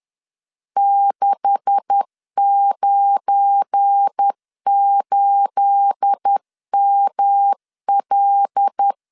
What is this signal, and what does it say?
Morse code, heard as a keyed tone, about 10 wpm, 800 Hz. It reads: 698ML
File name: LEZIONI / 9 286 10wpm 2735